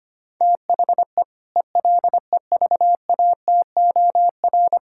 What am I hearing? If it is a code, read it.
Morse code, heard as a keyed tone, about 25 wpm, 700 Hz. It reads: THE ELEVATOR
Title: Sentences from Top 1600 Words 25wpm